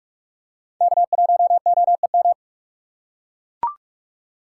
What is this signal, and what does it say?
Morse code, heard as a keyed tone, about 43 wpm, 690 Hz. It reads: K1OEM
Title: USA Call Signs - 5 Characters - 6x Character Spacing - 45wpm